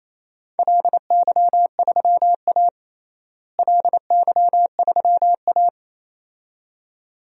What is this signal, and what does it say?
Morse code, heard as a keyed tone, about 27 wpm, 700 Hz. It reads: LY3A LY3A
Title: DXCC-100 Call Signs - 4 Characters - Repeated 2x - 28wpm